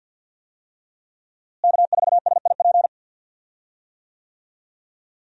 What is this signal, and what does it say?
Morse code, heard as a keyed tone, about 52 wpm, 700 Hz. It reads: K4SIP
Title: USA Call Signs - 5 Characters - Repeated 2x - 50wpm